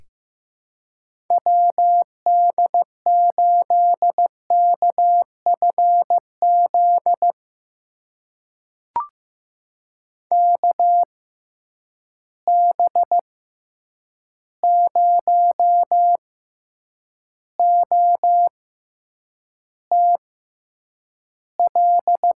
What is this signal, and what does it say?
Morse code, heard as a keyed tone, about 15 wpm, spaced as for 8 wpm, 700 Hz. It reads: WD8KFZ KB0OTL
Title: USA Call Signs - 6 Characters - 6x Character Spacing - 15wpm